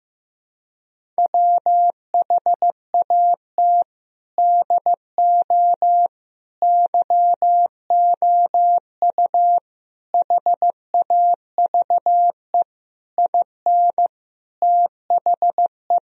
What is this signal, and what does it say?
Morse code, heard as a keyed tone, about 16 wpm, 700 Hz. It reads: WHAT DO YOU HAVE IN THE
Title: Sentences from Top 1000 Words - Review Part 2 - 15wpm